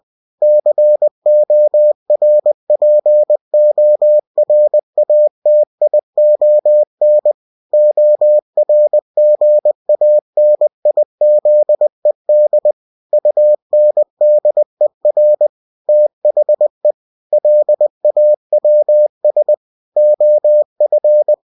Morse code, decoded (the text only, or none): CORPORATION ORGANIZED UNDER THE LAWS OF